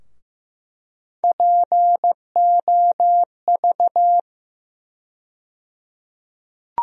Morse code, heard as a keyed tone, about 15 wpm, 700 Hz. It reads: POV
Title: USA Call Sign Postfixes - 3 Letters 15wpm